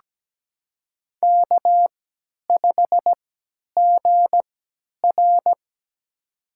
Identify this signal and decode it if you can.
Morse code, heard as a keyed tone, about 17 wpm, spaced as for 10 wpm, 700 Hz. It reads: K5GR
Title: USA Call Signs - 3x Character Spacing - 17wpm